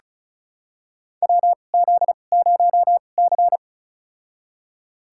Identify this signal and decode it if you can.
Morse code, heard as a keyed tone, about 34 wpm, spaced as for 25 wpm, 700 Hz. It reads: WZ0C
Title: USA Call Signs - 4 Characters - 2x Character Spacing - 35wpm